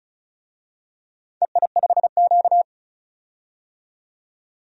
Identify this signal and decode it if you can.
Morse code, heard as a keyed tone, about 36 wpm, 700 Hz. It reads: EI5Q
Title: DXCC-100 Call Signs - 4 Characters - 35wpm